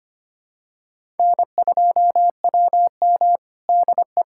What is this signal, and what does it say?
Morse code, heard as a keyed tone, about 25 wpm, 700 Hz. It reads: N2WM DE
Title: USA Callsign DE Callsign - 4 Characters - 25wpm